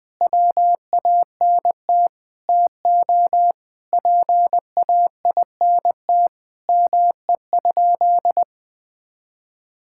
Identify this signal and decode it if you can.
Morse code, heard as a keyed tone, about 20 wpm, 700 Hz. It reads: WANT TO PAINT ME?
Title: Sentences from Top 1000 Words - Review Part 6 - 20wpm